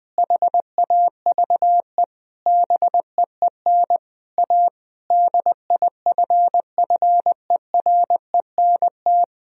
Morse code, heard as a keyed tone, about 20 wpm, 700 Hz. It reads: HAVE BEEN A DIFFERENT
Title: Sentences from Top 1000 Words - Review Part 3 - 20wpm